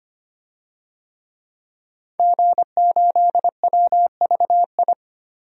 Morse code, heard as a keyed tone, about 25 wpm, 700 Hz. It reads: G8WVI
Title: DXCC-100 Call Signs - 5 Characters - 25wpm